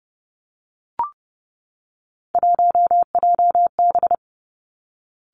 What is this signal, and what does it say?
Morse code, heard as a keyed tone, about 30 wpm, 700 Hz. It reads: 1JB
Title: Three Challenging Letter-Number Combinations 30wpm